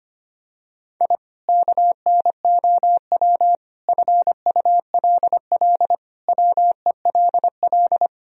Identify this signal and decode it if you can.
Morse code, heard as a keyed tone, about 25 wpm, 700 Hz. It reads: I KNOW FULL WELL
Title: Sentences from Top 1000 Words - Review Part 3 - 25wpm